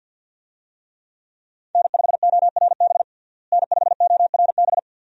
Morse code, heard as a keyed tone, about 52 wpm, 700 Hz. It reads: N5ORB N5ORB
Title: USA Call Signs - 5 Characters - Repeated 2x - 50wpm